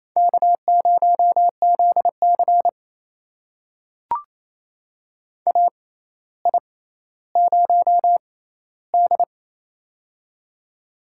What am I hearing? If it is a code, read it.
Morse code, heard as a keyed tone, about 28 wpm, spaced as for 14 wpm, 700 Hz. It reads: K0ZC AI0D
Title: USA Call Signs - 4 Characters - 6x Character Spacing - 28wpm